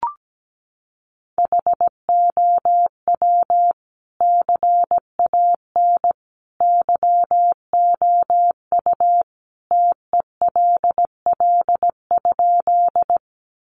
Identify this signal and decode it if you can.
Morse code, heard as a keyed tone, about 17 wpm, 700 Hz. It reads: HOW CAN YOU TELL?
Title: Sentences from Top 200 Words 17wpm